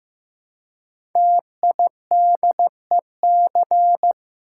Morse code, heard as a keyed tone, about 15 wpm, 700 Hz. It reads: TIDEC
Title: Code Groups - 5 Letters - 15wpm